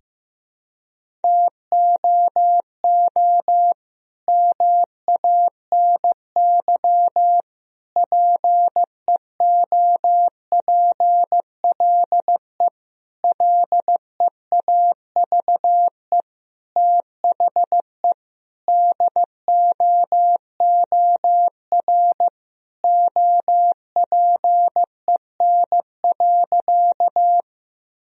Morse code, heard as a keyed tone, about 15 wpm, 700 Hz. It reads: TOO MANY PEOPLE LEAVE THE DOOR OPEN.
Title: Sentences from Top 1000 Words - Review Part 1 - 15wpm